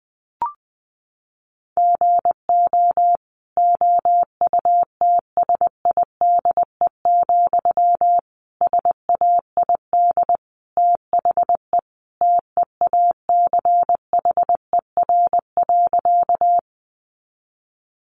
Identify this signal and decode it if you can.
Morse code, heard as a keyed tone, about 20 wpm, 700 Hz. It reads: GO OUTSIDE, SAID THE TEACHER.
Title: Sentences from Top 900 Words 20wpm